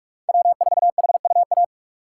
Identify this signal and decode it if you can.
Morse code, heard as a keyed tone, about 45 wpm, 700 Hz. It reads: W4HUA
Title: USA Call Signs - 5 Characters - 8x Character Spacing - 45wpm